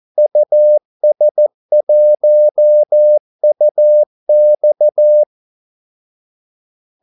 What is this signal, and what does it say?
Morse code, heard as a keyed tone, about 14 wpm, 600 Hz. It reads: US1UX
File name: Callsigns-14wpm